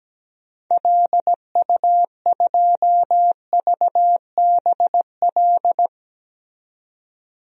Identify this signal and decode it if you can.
Morse code, heard as a keyed tone, about 17 wpm, 700 Hz. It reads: LU2VBL
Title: DXCC-100 Call Signs - 6 Characters - 17wpm